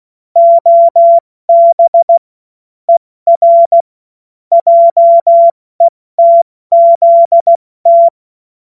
Morse code, heard as a keyed tone, about 16 wpm, spaced as for 14 wpm, 680 Hz. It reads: OB ER JETZT
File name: Morsekurs / Witze / Witz 29. Juni 2021, 70 F80